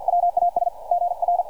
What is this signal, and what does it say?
Morse code, heard as a keyed tone, about 50 wpm, 700 Hz, with radio noise. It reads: WAS GR
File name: archived / CRSnet